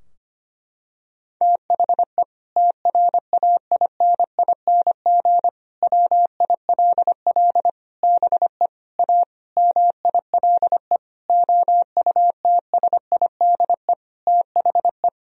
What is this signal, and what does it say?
Morse code, heard as a keyed tone, about 25 wpm, 700 Hz. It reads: THE TRAINING WILL BE A MILE OUTSIDE THE
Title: Sentences from Top 1000 Words - Review Part 4 - 25wpm